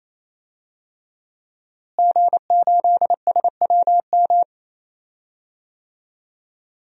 Morse code, heard as a keyed tone, about 28 wpm, 700 Hz. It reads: G8SWM
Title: DXCC-100 Call Signs - 5 Characters - Repeated 3x - 28wpm